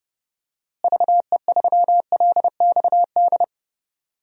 Morse code, heard as a keyed tone, about 30 wpm, 700 Hz. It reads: VE3LXD
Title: DXCC-100 Call Signs - 6 Characters - Repeated 3x - 30wpm